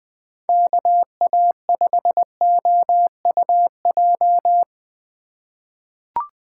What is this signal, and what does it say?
Morse code, heard as a keyed tone, about 21 wpm, 700 Hz. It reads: KA5OUJ
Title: USA Call Signs - 6 Characters - 4x Character Spacing - 20wpm